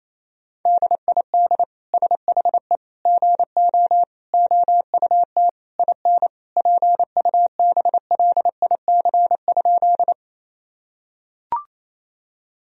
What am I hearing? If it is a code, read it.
Morse code, heard as a keyed tone, about 27 wpm, 700 Hz. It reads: DID SHE GO OUT IN PUBLIC?
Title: Sentences from Top 1000 Words - Review Part 3 - 28wpm